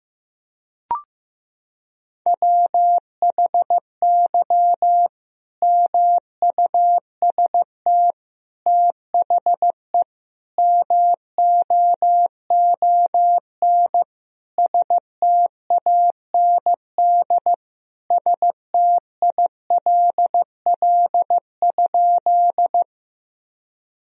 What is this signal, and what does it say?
Morse code, heard as a keyed tone, about 15 wpm, 700 Hz. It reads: WHY MUST THE MOON STAND STILL?
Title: Sentences from Top 1000 Words - Review Part 3 - 15wpm